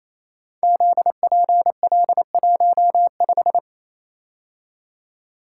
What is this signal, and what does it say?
Morse code, heard as a keyed tone, about 27 wpm, 700 Hz. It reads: ZPL15
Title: Code Groups - 5 Letter:Numbers - 28wpm